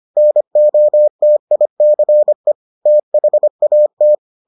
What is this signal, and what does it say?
Morse code, heard as a keyed tone, about 25 wpm, 600 Hz. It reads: NOTICE THAT